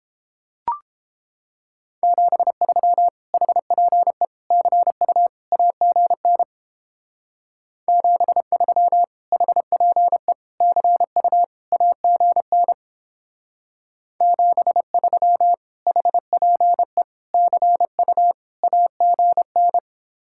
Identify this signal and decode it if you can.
Morse code, heard as a keyed tone, about 29 wpm, 700 Hz. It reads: 73 HPE CU AGN 73 HPE CU AGN 73 HPE CU AGN